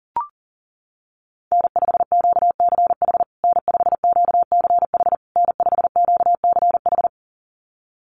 Morse code, heard as a keyed tone, about 37 wpm, 700 Hz. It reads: N5QCH N5QCH N5QCH
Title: USA Call Signs Repeated 3x 40wpm